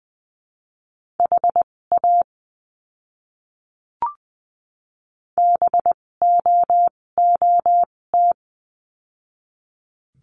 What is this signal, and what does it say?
Morse code, heard as a keyed tone, about 20 wpm, spaced as for 16 wpm, 700 Hz. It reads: HA BOOT
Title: 256 Lesson 42a - Words 12-20wpm